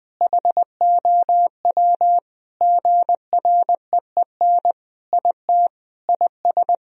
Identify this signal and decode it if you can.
Morse code, heard as a keyed tone, about 21 wpm, 700 Hz. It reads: HOW GREEN IT IS
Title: Sentences from Top 1000 Words - Review Part 2 - 20wpm